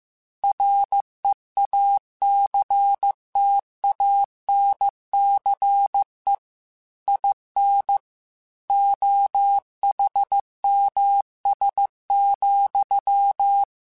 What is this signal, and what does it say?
Morse code, heard as a keyed tone, about 15 wpm, 800 Hz. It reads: REACTANCE IN OHMS,